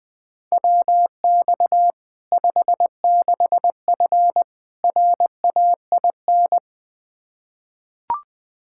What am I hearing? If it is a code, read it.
Morse code, heard as a keyed tone, about 20 wpm, 700 Hz. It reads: WX 56F RAIN
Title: Practice QSO - Exchange 2 - Info - 20wpm